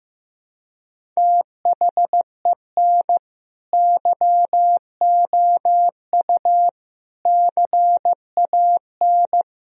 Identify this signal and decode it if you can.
Morse code, heard as a keyed tone, about 15 wpm, 700 Hz. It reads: THEN YOU CAN
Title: Sentences from Top 1000 Words - Review Part 1 - 15wpm